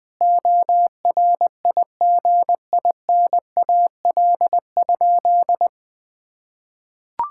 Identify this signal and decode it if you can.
Morse code, heard as a keyed tone, about 20 wpm, 700 Hz. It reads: ORIGINAL?
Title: Sentences from Top 1000 Words - Review Part 6 - 20wpm